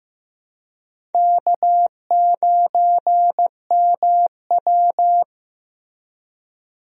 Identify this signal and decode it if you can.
Morse code, heard as a keyed tone, about 15 wpm, 700 Hz. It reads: K9MW
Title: USA Call Signs - 4 Characters - Repeated 2x - 15wpm